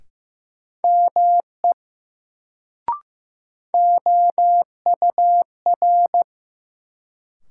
Morse code, heard as a keyed tone, about 15 wpm, 700 Hz. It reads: ME OUR
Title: Sentences from Top 200 Words - New Word Drill - 15wpm